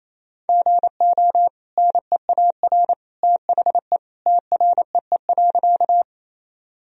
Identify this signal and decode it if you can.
Morse code, heard as a keyed tone, about 28 wpm, 700 Hz. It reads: GO NEAR THE TREE.
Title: Sentences from Top 1000 Words - Review Part 1 - 28wpm